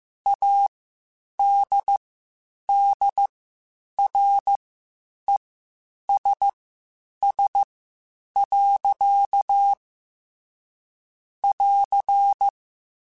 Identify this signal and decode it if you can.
Morse code, heard as a keyed tone, about 15 wpm, spaced as for 8 wpm, 800 Hz. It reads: ADDRESS. <AR>
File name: Spaced / Chuckles Spaced-mp3s / Temptation sp @13